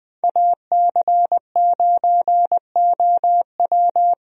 Morse code, heard as a keyed tone, about 20 wpm, 700 Hz. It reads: AC9OW
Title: USA Call Signs - 5 Characters - 20wpm